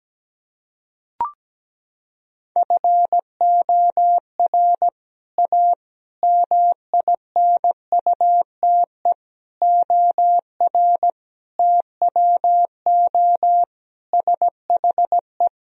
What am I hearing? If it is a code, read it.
Morse code, heard as a keyed tone, about 17 wpm, 700 Hz. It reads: FOR A MINUTE OR TWO SHE
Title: Sentences from Top 1000 Words - Review Part 1 - 17wpm